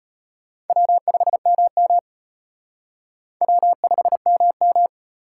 Morse code, heard as a keyed tone, about 33 wpm, 700 Hz. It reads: W5MM W5MM